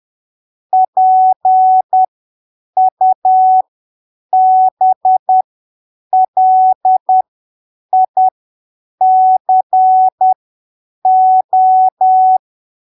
Morse code, heard as a keyed tone, about 10 wpm, spaced as for 7 wpm, 750 Hz. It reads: PUBLICO
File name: arquivos / texto / texto 01 10ppm